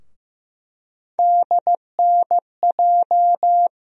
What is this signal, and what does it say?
Morse code, heard as a keyed tone, about 15 wpm, 700 Hz. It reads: DNJ
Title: USA Call Sign Postfixes - 3 Letters 15wpm